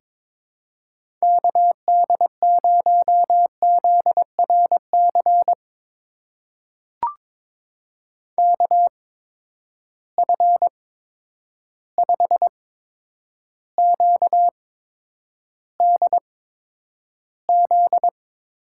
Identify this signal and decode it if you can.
Morse code, heard as a keyed tone, about 21 wpm, spaced as for 9 wpm, 700 Hz. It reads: KD0ZRC KF5QDZ
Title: USA Call Signs - 6 Characters - 8x Character Spacing - 22wpm